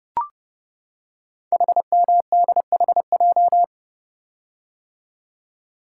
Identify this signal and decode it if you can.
Morse code, heard as a keyed tone, about 30 wpm, 700 Hz. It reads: HMDHJ
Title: Code Groups - 5 Letters - 30wpm